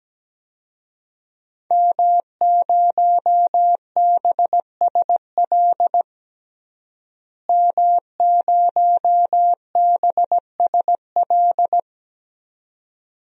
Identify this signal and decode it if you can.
Morse code, heard as a keyed tone, about 17 wpm, 700 Hz. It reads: M0BSL M0BSL
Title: DXCC-100 Call Signs - 5 Characters - Repeated 2x - 17wpm